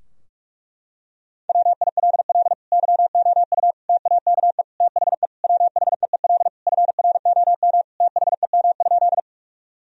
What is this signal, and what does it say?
Morse code, heard as a keyed tone, about 46 wpm, 700 Hz. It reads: WILL YOU TAKE THE WHEEL FROM THEM?
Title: Sentences from Top 1000 Words - Review Part 3 - 45wpm